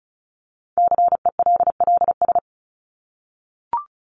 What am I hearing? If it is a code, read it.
Morse code, heard as a keyed tone, about 34 wpm, 700 Hz. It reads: CELLS
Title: ES:S Suffix - Top 100 Words with Base Words 35wpm